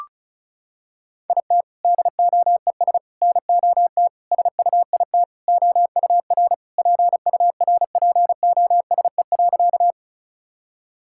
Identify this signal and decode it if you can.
Morse code, heard as a keyed tone, about 35 wpm, 700 Hz. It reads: IT DOES NOT SUIT OUR PURPOSE.
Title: Sentences from Top 1600 Words 35wpm